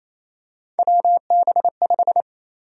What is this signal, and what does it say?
Morse code, heard as a keyed tone, about 28 wpm, 700 Hz. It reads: WB5
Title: USA Call Sign Prefixes - 28wpm